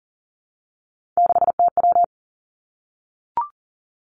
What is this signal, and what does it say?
Morse code, heard as a keyed tone, about 42 wpm, 690 Hz. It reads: 6TW
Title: Three Challenging Letter-Number Combinations 40wpm